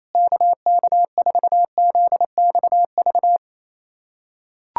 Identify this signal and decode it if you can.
Morse code, heard as a keyed tone, about 28 wpm, 700 Hz. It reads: KK4ZXV
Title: USA Callsign DE Callsign 28wpm